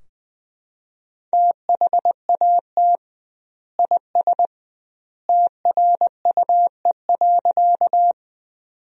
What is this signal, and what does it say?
Morse code, heard as a keyed tone, about 20 wpm, 700 Hz. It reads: THAT IS TRUE.
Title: 2x Word-Spacing - Sentences from Top 400 Words - 20wpm